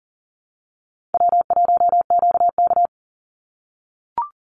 Morse code, heard as a keyed tone, about 42 wpm, 700 Hz. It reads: W1QK
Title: Easy Call Signs - 40wpm